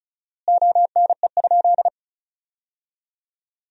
Morse code, heard as a keyed tone, about 36 wpm, 700 Hz. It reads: ONE?
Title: 4x Word-Spacing - Sentences from Top 300 Words - 35wpm